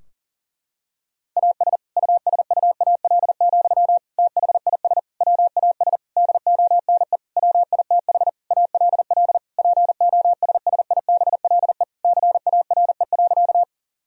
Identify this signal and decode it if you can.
Morse code, heard as a keyed tone, about 40 wpm, 700 Hz. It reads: AS USUAL, THIS WAS DONE WITH ALL POSSIBLE CARE.
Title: Sentences from Top 1000 Words - Review Part 2 - 40wpm